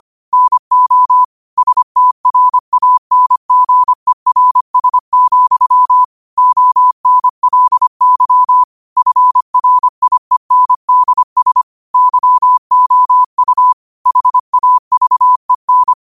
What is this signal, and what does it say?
Morse code, heard as a keyed tone, about 25 wpm, 1000 Hz. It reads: NO STRANGERS, ONLY FRIENDS YOU HAVEN
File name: qotdfiles / 25wpm